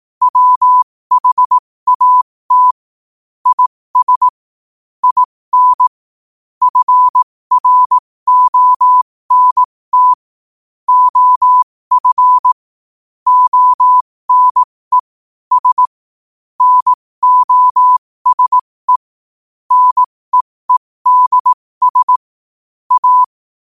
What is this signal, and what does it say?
Morse code, heard as a keyed tone, about 18 wpm, spaced as for 15 wpm, 1000 Hz. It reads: WHAT IS IN FRONT OF ONE S NOSE NEEDS A
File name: qotdfiles / 15wpm